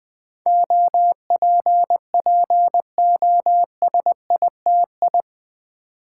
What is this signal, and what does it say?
Morse code, heard as a keyed tone, about 20 wpm, 700 Hz. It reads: OPPOSITI
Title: Build A Word - 4 Letters At A Time - 10-Letter Words - 20wpm